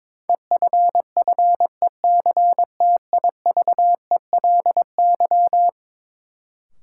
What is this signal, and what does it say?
Morse code, heard as a keyed tone, about 22 wpm, 700 Hz. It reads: EFFECTIVELY
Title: Sentences from Top 1400 Words - New Word Drill - 22wpm